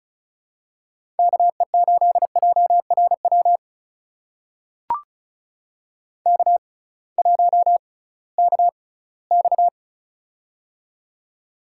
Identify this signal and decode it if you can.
Morse code, heard as a keyed tone, about 36 wpm, spaced as for 21 wpm, 700 Hz. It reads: KE8JRW K1KX
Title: USA Call Signs - 6x Character Spacing - 35wpm